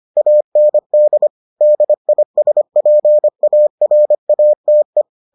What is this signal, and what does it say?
Morse code, heard as a keyed tone, about 25 wpm, 600 Hz. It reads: AND DISPARATE